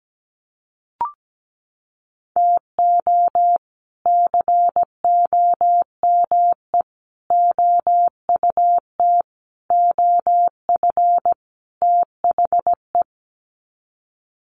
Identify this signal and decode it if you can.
Morse code, heard as a keyed tone, about 17 wpm, 700 Hz. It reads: TO COME OUT OF THE
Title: Sets of 5 Words Encore2 17wpm